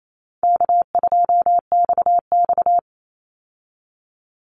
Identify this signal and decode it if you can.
Morse code, heard as a keyed tone, about 28 wpm, 700 Hz. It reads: K2XX
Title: Easy Call Signs - 28wpm